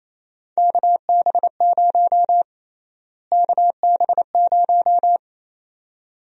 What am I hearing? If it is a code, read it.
Morse code, heard as a keyed tone, about 28 wpm, 700 Hz. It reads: KB0 KB0
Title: USA Call Sign Prefixes - Repeated 2x - 28wpm